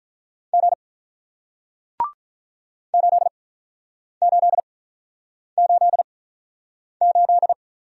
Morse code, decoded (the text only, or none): G 8888